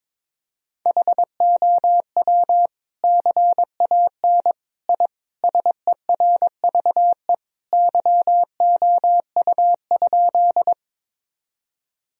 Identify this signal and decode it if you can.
Morse code, heard as a keyed tone, about 21 wpm, 700 Hz. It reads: HOW CAN I SERVE YOU?
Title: Sentences from Top 1000 Words - Review Part 2 - 22wpm